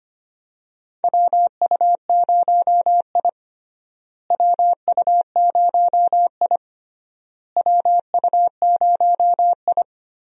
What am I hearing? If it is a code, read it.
Morse code, heard as a keyed tone, about 25 wpm, 700 Hz. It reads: WU0I WU0I WU0I
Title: USA Call Signs - 4 Characters - Repeated 3x - 25wpm